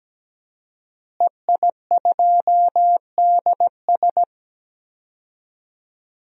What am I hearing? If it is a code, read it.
Morse code, heard as a keyed tone, about 17 wpm, 700 Hz. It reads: EI2DS
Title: DXCC-100 Call Signs - 5 Characters - Repeated 3x - 17wpm